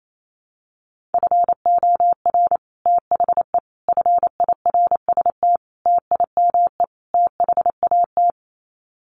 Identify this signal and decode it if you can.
Morse code, heard as a keyed tone, about 28 wpm, 700 Hz. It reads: FOR THE FIRST TIME THAT
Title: Sets of 5 Words Encore 28wpm